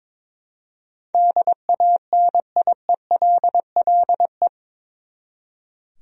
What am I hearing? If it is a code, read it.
Morse code, heard as a keyed tone, about 21 wpm, 700 Hz. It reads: DANIELLE
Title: USA Common Names - 22wpm